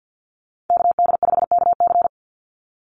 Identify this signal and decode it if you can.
Morse code, heard as a keyed tone, about 50 wpm, 700 Hz. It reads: KD5KC
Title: NA SOTA Chaser Callsigns 50wpm